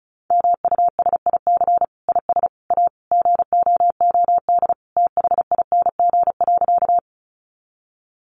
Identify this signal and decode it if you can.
Morse code, heard as a keyed tone, about 34 wpm, 700 Hz. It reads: MUSIC IS A GOOD THING.
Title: Sentences from Top 300 Words 35wpm